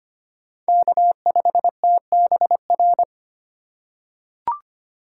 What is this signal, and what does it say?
Morse code, heard as a keyed tone, about 25 wpm, 700 Hz. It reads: K5TBR
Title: USA Call Signs - 5 Characters - 3x Character Spacing - 25wpm